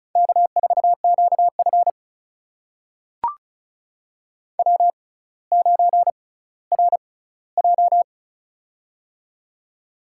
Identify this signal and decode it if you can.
Morse code, heard as a keyed tone, about 34 wpm, spaced as for 18 wpm, 700 Hz. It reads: K4QF W9RJ
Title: USA Call Signs - 4 Characters - 6x Character Spacing - 35wpm